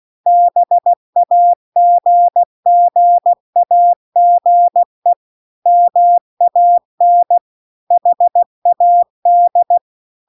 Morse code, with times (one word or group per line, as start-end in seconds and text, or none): 0.3-5.1 BAGGAGE
5.7-7.4 MAN
7.9-9.8 HAD